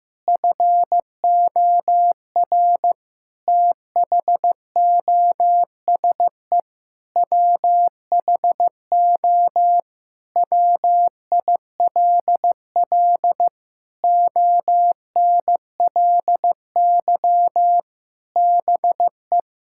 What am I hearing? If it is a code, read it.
Morse code, heard as a keyed tone, about 15 wpm, 700 Hz. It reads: FOR THOSE WHO WILL ONLY BE
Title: Sentences from Top 1000 Words - Review Part 6 - 15wpm